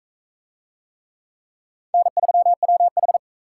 Morse code, heard as a keyed tone, about 36 wpm, 700 Hz. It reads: N3WH